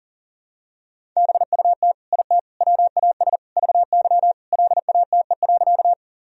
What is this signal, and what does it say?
Morse code, heard as a keyed tone, about 40 wpm, 700 Hz. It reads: BUT IT WAS VY LATE.
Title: Sentences from Top 300 Words and Abbreviations 40wpm